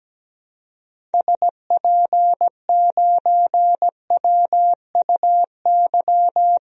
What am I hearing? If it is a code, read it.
Morse code, heard as a keyed tone, about 17 wpm, 700 Hz. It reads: SP9WUY
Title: DXCC-100 Call Signs - 6 Characters - Repeated 2x - 17wpm